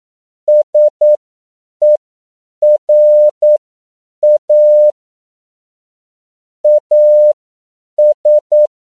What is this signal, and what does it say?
Morse code, heard as a keyed tone, about 9 wpm, spaced as for 7 wpm, 600 Hz. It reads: SERA AS
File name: CW / 10PPM